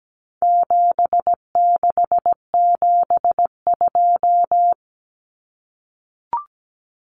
Morse code, heard as a keyed tone, about 17 wpm, 700 Hz. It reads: 7672
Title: Common Cut Numbers - 4-Digit 17wpm